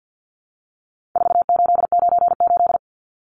Morse code, heard as a keyed tone, about 50 wpm, 700 Hz. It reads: 4898
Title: Numbers - 4-Digit 50wpm